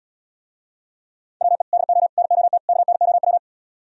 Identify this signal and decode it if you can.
Morse code, heard as a keyed tone, about 77 wpm, 700 Hz. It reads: WE DO NOT KNOW